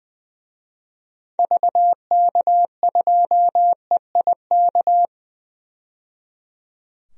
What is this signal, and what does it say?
Morse code, heard as a keyed tone, about 20 wpm, 700 Hz. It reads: VK2EIK
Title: DXCC-100 Call Signs - 6 Characters - 20wpm